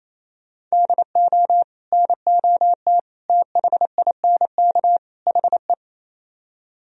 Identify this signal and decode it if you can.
Morse code, heard as a keyed tone, about 36 wpm, 700 Hz. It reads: DO NOT THINK HE